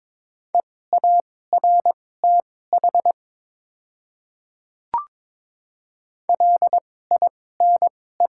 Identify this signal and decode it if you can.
Morse code, heard as a keyed tone, about 22 wpm, spaced as for 16 wpm, 700 Hz. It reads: EARTH LINE
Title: 2x Character Spacing - Top 500 Words - 22wpm